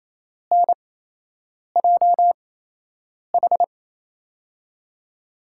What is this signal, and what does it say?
Morse code, heard as a keyed tone, about 28 wpm, spaced as for 8 wpm, 700 Hz. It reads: NJH
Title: USA Call Signs - 5 Characters - 8x Character Spacing - 28wpm